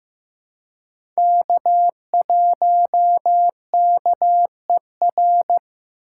Morse code, heard as a keyed tone, about 15 wpm, 700 Hz. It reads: K1KER
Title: USA Call Signs - 5 Characters - 3x Character Spacing - 15wpm